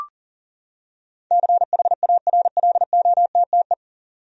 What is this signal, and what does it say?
Morse code, heard as a keyed tone, about 40 wpm, 700 Hz. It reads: CHARLOTTE
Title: USA Common Names - 40wpm